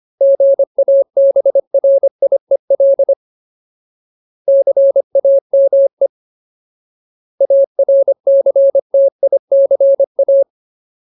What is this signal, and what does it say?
Morse code, heard as a keyed tone, about 25 wpm, 550 Hz. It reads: GABRIEL CAME ARCTICA
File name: Ultimate Word List / 25WPM Ultimate